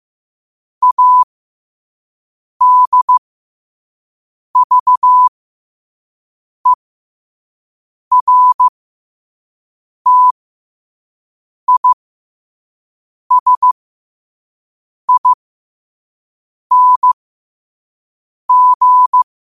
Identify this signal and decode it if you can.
Morse code, heard as a keyed tone, about 15 wpm, spaced as for 5 wpm, 1000 Hz. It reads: ADVERTISING
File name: qotdfiles / 5wpm